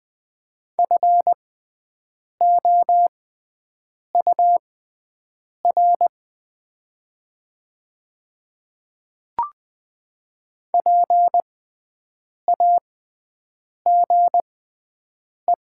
Morse code, heard as a keyed tone, about 20 wpm, spaced as for 7 wpm, 700 Hz. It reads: FOUR PAGE
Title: 6x Character Spacing - Top 300 Words - 20wpm